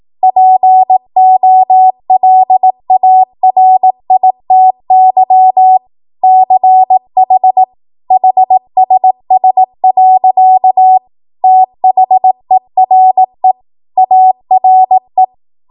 Morse code, heard as a keyed tone, about 18 wpm, 750 Hz. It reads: POLARITY CH HSS. THERE ARE
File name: Morse / 18WPM